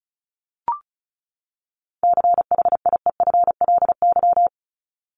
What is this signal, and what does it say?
Morse code, heard as a keyed tone, about 34 wpm, 700 Hz. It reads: CHIEFLY
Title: LY Suffix - Top 100 Words with Base Words 35wpm